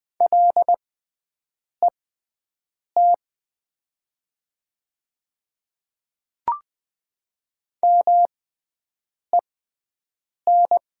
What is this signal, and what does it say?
Morse code, heard as a keyed tone, about 20 wpm, spaced as for 7 wpm, 700 Hz. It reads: LET MEN
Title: 6x Character Spacing - Top 300 Words - 20wpm